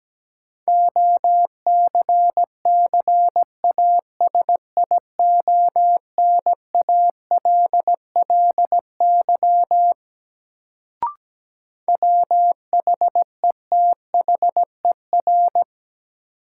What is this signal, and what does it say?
Morse code, heard as a keyed tone, about 17 wpm, 700 Hz. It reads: OCCASIONALLY WHETHER
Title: Sentences from Top 1300 Words - New Word Drill Alt - 17wpm